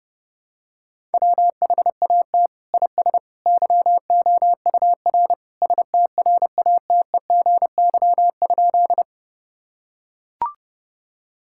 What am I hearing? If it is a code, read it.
Morse code, heard as a keyed tone, about 30 wpm, 700 Hz. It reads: WHAT IS YOUR STRATEGY?
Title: Sentences from Top 1000 Words - Review Part 5 - 30wpm